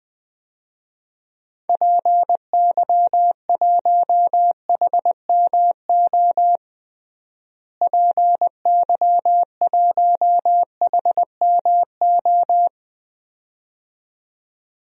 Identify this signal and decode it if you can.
Morse code, heard as a keyed tone, about 20 wpm, 700 Hz. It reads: PY1HMO PY1HMO
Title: DXCC-100 Call Signs - 6 Characters - Repeated 2x - 20wpm